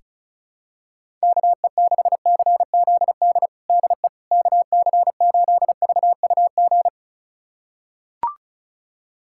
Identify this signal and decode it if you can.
Morse code, heard as a keyed tone, about 34 wpm, 700 Hz. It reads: KE6CZD DE KC8VUG
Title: USA Callsign DE Callsign - 6 Characters - 35wpm